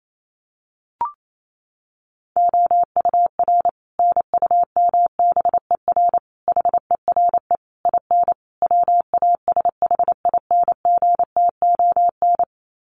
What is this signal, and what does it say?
Morse code, heard as a keyed tone, about 28 wpm, 700 Hz. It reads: OUR NUMBER HERE IN WASHINGTON
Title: Sets of 5 Words Encore2 28wpm